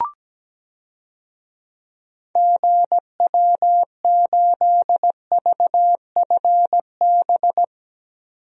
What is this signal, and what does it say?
Morse code, heard as a keyed tone, about 17 wpm, 700 Hz. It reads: GW8VFB
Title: DXCC-100 Call Signs - 6 Characters - 17wpm